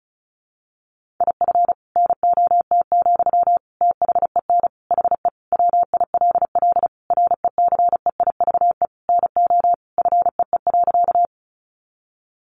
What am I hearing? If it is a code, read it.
Morse code, heard as a keyed tone, about 34 wpm, 700 Hz. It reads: IF NOT, THEN HE WILL RECEIVE NO FEE.
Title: Sentences from Top 1500 Words 35wpm